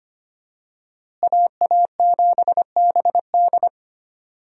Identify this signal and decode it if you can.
Morse code, heard as a keyed tone, about 25 wpm, 690 Hz. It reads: AA7BD
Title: USA Call Signs - 5 Characters - Repeated 2x - 25wpm